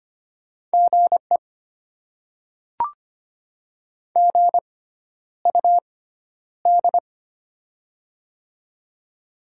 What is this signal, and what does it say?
Morse code, heard as a keyed tone, about 25 wpm, spaced as for 11 wpm, 700 Hz. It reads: GE GUD
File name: Top 20 QSO Elements - 6x Character Spacing - 25wpm